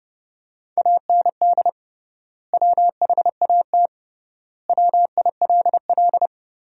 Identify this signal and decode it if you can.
Morse code, heard as a keyed tone, about 30 wpm, 700 Hz. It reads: AND WHAT WILL
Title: 3x Word-Spacing - Sentences from Top 200 Words - 30wpm